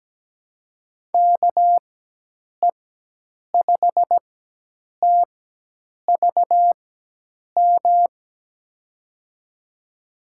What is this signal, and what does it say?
Morse code, heard as a keyed tone, about 17 wpm, spaced as for 8 wpm, 700 Hz. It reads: KE5TVM
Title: USA Call Signs - 6 Characters - 4x Character Spacing - 17wpm